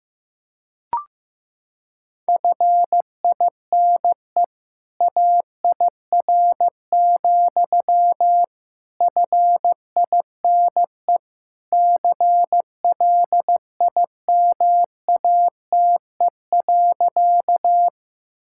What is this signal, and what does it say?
Morse code, heard as a keyed tone, about 15 wpm, 700 Hz. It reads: FINE AIR, FINE CLIMATE.
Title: Sentences from Top 1600 Words 15wpm